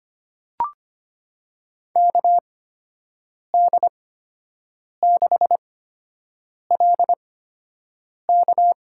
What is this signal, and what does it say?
Morse code, heard as a keyed tone, about 25 wpm, spaced as for 7 wpm, 700 Hz. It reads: KD6LK
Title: USA Call Signs - 5 Characters - 8x Character Spacing - 25wpm